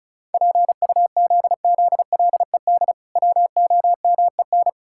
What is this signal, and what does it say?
Morse code, heard as a keyed tone, about 36 wpm, 700 Hz. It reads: PUZZLED WOMEN
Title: Pangram Challenge v2 - 35wpm